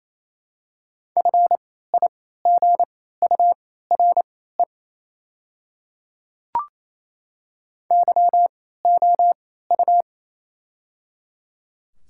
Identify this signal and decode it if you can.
Morse code, heard as a keyed tone, about 28 wpm, spaced as for 16 wpm, 700 Hz. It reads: FIGURE YOU
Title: 3x Character Spacing - Top 500 Words - 28wpm